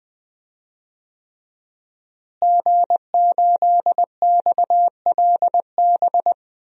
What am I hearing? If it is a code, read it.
Morse code, heard as a keyed tone, about 20 wpm, 700 Hz. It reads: G8XLB
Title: DXCC-100 Call Signs - 5 Characters - Repeated 3x - 20wpm